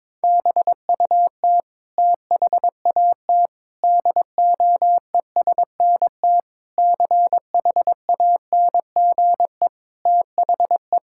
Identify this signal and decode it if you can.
Morse code, heard as a keyed tone, about 22 wpm, 700 Hz. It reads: BUT THAT DOESNT CHANGE THE
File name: practice / 22wpm / Sets of 7 Words - 22wpm